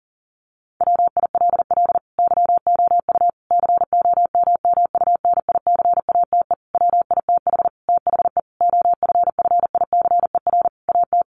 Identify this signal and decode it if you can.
Morse code, heard as a keyed tone, about 40 wpm, 700 Hz. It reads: WILL YOU COMMUNICATE WITH THE OFFICER AT
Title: Sentences from Top 1300 Words 40wpm